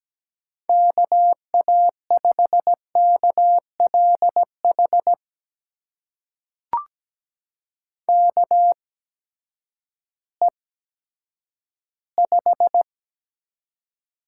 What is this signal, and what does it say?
Morse code, heard as a keyed tone, about 17 wpm, spaced as for 10 wpm, 700 Hz. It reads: KA5KLH KE5
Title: USA Call Signs - 8x Character Spacing - 17wpm